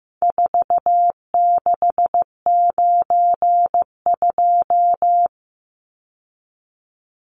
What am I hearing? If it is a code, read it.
Morse code, heard as a keyed tone, about 15 wpm, 700 Hz. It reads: 4692
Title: Numbers - 4-Digit 15wpm